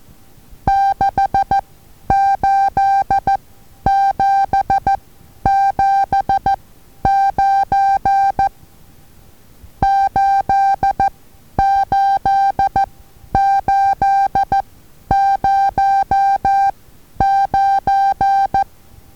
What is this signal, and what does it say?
Morse code, heard as a keyed tone, about 14 wpm, spaced as for 10 wpm, 790 Hz. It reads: 68779 88809
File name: Taviro HA6VA A